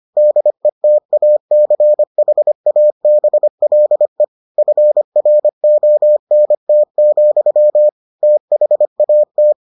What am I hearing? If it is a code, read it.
Morse code, heard as a keyed tone, about 25 wpm, 600 Hz. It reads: DETACHABLE FRONT, THAT